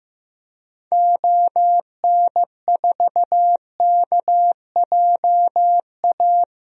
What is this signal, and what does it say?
Morse code, heard as a keyed tone, about 15 wpm, 700 Hz. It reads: ON4KJA
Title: DXCC-100 Call Signs - 6 Characters - Repeated 2x - 15wpm